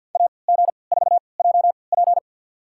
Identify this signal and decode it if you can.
Morse code, heard as a keyed tone, about 53 wpm, spaced as for 29 wpm, 700 Hz. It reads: AG4JP
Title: USA Call Signs - 3x Character Spacing - 50wpm